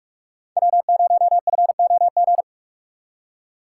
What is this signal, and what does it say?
Morse code, heard as a keyed tone, about 45 wpm, 700 Hz. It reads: W0FOG
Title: USA Call Signs - 5 Characters - 4x Character Spacing - 45wpm